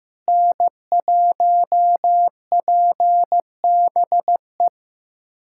Morse code, heard as a keyed tone, about 15 wpm, 700 Hz. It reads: N1PBE
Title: USA Callsign DE Callsign - 5 Characters - 15wpm